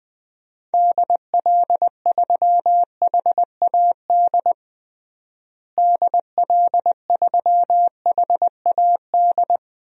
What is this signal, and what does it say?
Morse code, heard as a keyed tone, about 20 wpm, 700 Hz. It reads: DL3HAD DL3HAD
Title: DXCC-100 Call Signs - 6 Characters - Repeated 2x - 20wpm